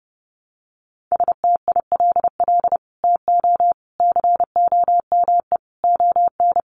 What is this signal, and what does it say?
Morse code, heard as a keyed tone, about 30 wpm, 700 Hz. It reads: STILL TO COME ON
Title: Sets of 5 Words Encore2 30wpm